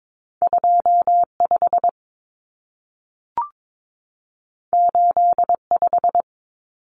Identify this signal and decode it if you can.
Morse code, heard as a keyed tone, about 22 wpm, 700 Hz. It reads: 25 85
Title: Numbers - 2-Digit 22wpm